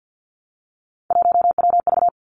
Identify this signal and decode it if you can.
Morse code, heard as a keyed tone, about 50 wpm, 700 Hz. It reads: 1WV
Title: Three Challenging Letter-Number Combinations 50wpm